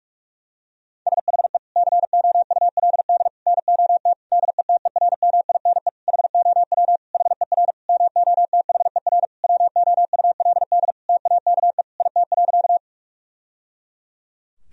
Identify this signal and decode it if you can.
Morse code, heard as a keyed tone, about 47 wpm, 700 Hz. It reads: SHE COULD NOT DETERMINE HOW HER MOTHER WOULD TAKE IT.
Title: Sentences from Top 1000 Words - Review Part 4 - 45wpm